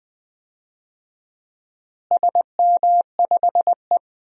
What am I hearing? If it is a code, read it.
Morse code, heard as a keyed tone, about 20 wpm, 700 Hz. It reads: SM5E
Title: DXCC-100 Call Signs - 4 Characters - Repeated 2x - 20wpm